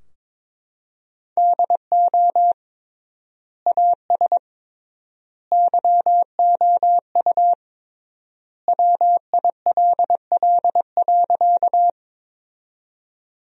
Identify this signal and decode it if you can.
Morse code, heard as a keyed tone, about 21 wpm, 700 Hz. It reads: DO AS YOU WILL.
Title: 3x Word-Spacing - Sentences from Top 100 Words - 22wpm